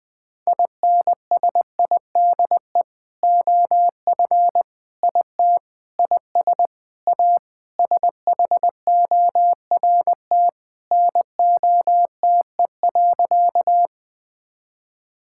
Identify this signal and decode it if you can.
Morse code, heard as a keyed tone, about 20 wpm, 700 Hz. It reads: INSIDE OF IT IS A SHORT NOTE.
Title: Sentences from Top 1000 Words - Review Part 6 - 20wpm